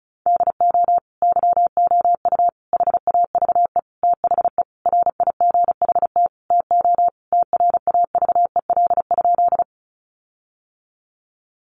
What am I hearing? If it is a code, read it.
Morse code, heard as a keyed tone, about 34 wpm, 700 Hz. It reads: DO YOU HAVE THE RIGHT TO TRAVEL?
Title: Sentences from Top 1400 Words 35wpm